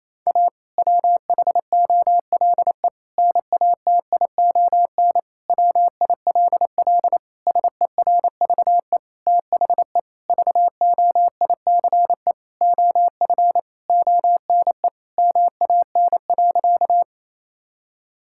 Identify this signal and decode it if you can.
Morse code, heard as a keyed tone, about 28 wpm, 700 Hz. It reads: A WHOLE NATION WILL SERVE THE VOICE OF ONE MAN.
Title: Sentences from Top 1000 Words - Review Part 6 - 28wpm